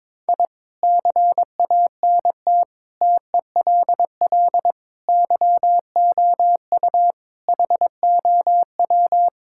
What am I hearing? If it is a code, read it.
Morse code, heard as a keyed tone, about 22 wpm, 700 Hz. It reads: I CANT TELL YOU HOW
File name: practice / 22wpm / Sets of 8 Words - 22wpm